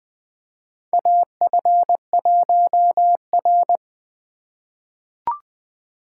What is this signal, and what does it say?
Morse code, heard as a keyed tone, about 20 wpm, 700 Hz. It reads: AF1R
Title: USA Call Signs - 4 Characters - 2x Character Spacing - 20wpm